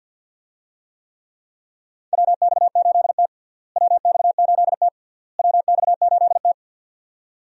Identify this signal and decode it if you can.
Morse code, heard as a keyed tone, about 51 wpm, 700 Hz. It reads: WX8T WX8T WX8T
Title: USA Call Signs - 4 Characters - Repeated 3x - 50wpm